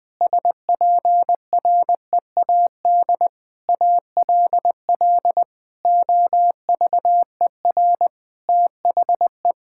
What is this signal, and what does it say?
Morse code, heard as a keyed tone, about 20 wpm, 700 Hz. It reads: SPREAD ALL OVER THE
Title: Sentences with 3-Word Phrases - Spoken Beginning - 20wpm